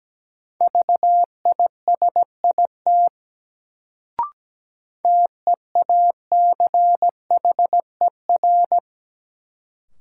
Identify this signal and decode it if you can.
Morse code, heard as a keyed tone, about 17 wpm, 700 Hz. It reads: VISIT TEACHER
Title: Sentences from Top 900 Words - New Word Drill - 17wpm